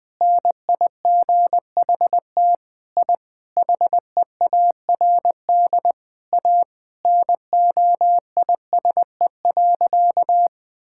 Morse code, heard as a keyed tone, about 20 wpm, 700 Hz. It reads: NIGHT I HEARD A NOISE.
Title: Sentences with 5-Word Phrases - 20wpm